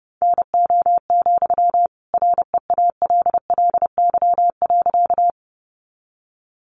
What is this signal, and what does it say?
Morse code, heard as a keyed tone, about 30 wpm, 700 Hz. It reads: NO, REALLY.
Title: Sentences from Top 600 Words 30wpm